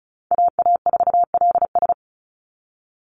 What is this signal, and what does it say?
Morse code, heard as a keyed tone, about 34 wpm, 700 Hz. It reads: AA4LS
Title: Call Signs Repeated 2x - 35wpm